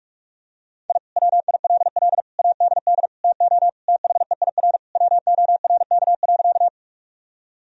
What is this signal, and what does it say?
Morse code, heard as a keyed tone, about 43 wpm, 700 Hz. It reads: I WILL ADD TO THEIR WORK.
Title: Sentences from Top 1000 Words - Review Part 1 - 45wpm